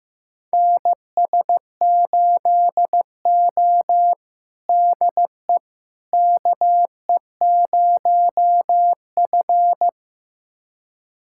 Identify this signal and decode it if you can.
Morse code, heard as a keyed tone, about 15 wpm, 700 Hz. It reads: NS8O DE KE0F
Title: USA Callsign DE Callsign - 4 Characters - 15wpm